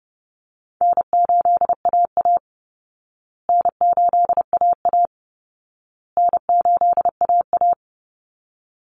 Morse code, heard as a keyed tone, about 30 wpm, 700 Hz. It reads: N8AA N8AA N8AA
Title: Easy Call Signs Repeated 3x - 30wpm